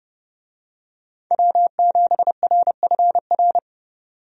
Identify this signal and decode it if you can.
Morse code, heard as a keyed tone, about 30 wpm, 700 Hz. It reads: W7RFR
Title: USA Call Signs - 5 Characters - 30wpm